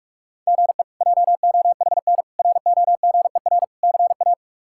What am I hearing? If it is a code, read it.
Morse code, heard as a keyed tone, about 46 wpm, 700 Hz. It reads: GE JOHN ROGER CA
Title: K1USN SST QSO Practice - 45wpm